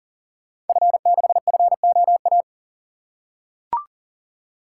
Morse code, heard as a keyed tone, about 41 wpm, 690 Hz. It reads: F6FOA
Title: Code Groups - 5 Letter:Numbers - 40wpm